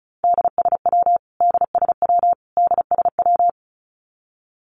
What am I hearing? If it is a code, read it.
Morse code, heard as a keyed tone, about 34 wpm, 700 Hz. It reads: DSW DSW DSW
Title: Abbreviations Repeated 3x - 35wpm